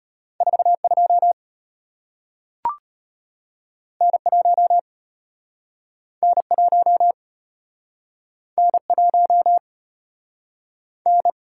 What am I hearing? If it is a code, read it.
Morse code, heard as a keyed tone, about 33 wpm, 690 Hz. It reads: 42 N1 N1 N1 N